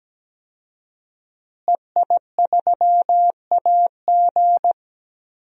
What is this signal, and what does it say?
Morse code, heard as a keyed tone, about 17 wpm, 700 Hz. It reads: EI3AG
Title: DXCC-100 Call Signs - 5 Characters - Repeated 3x - 17wpm